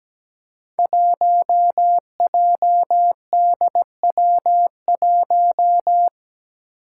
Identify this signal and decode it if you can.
Morse code, heard as a keyed tone, about 17 wpm, 700 Hz. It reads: 1JDW1
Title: Five Challenging Letter-Number Combinations 17wpm